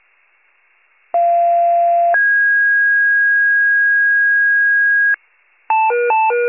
Two-tone page, heard as a one-second tone, then a three-second tone, about 690 then 1650 Hz. It is heard from 1.1 to 5.2 s.
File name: pager / 41 RESCUE DUTY / wcalert